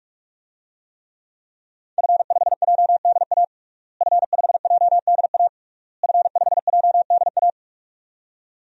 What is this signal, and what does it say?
Morse code, heard as a keyed tone, about 46 wpm, 700 Hz. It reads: F5JDA F5JDA F5JDA
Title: DXCC-100 Call Signs - 5 Characters - Repeated 3x - 45wpm